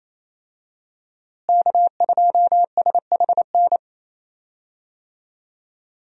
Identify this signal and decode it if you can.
Morse code, heard as a keyed tone, about 28 wpm, 700 Hz. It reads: K2SHN
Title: USA Call Signs - 5 Characters - Repeated 2x - 28wpm